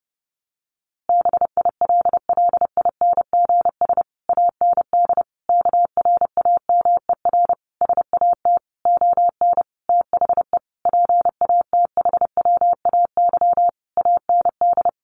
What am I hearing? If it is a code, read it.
Morse code, heard as a keyed tone, about 30 wpm, 700 Hz. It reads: BILLINGS AND KRAMER SAT ON THE PATHWAY AND
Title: A Sound Of Thunder 30wpm